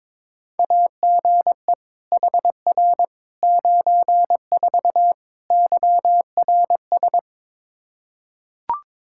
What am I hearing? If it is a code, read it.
Morse code, heard as a keyed tone, about 22 wpm, 700 Hz. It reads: AGE HR 94 YRS
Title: Practice QSO - Exchange 3 - More Info - 22wpm